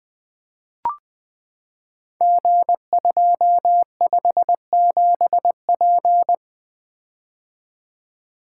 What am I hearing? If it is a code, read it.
Morse code, heard as a keyed tone, about 20 wpm, 700 Hz. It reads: G257P
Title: Code Groups - 5 Letter:Numbers - 20wpm